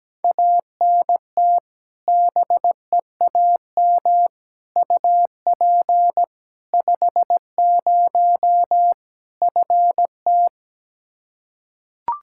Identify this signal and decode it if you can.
Morse code, heard as a keyed tone, about 17 wpm, 700 Hz. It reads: ANT BEAM UP 50 FT
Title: Practice QSO - Exchange 2 - Info - 17wpm